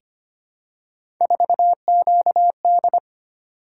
Morse code, heard as a keyed tone, about 25 wpm, 700 Hz. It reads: 4QD
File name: Three Letter-Number Combinations - 25wpm